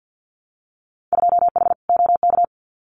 Morse code, heard as a keyed tone, about 50 wpm, 700 Hz. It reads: 2H OK
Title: Winter Field Day Exchanges Encore 50wpm